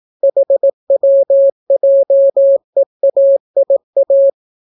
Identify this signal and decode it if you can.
Morse code, heard as a keyed tone, about 18 wpm, 550 Hz. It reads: HWJEAIA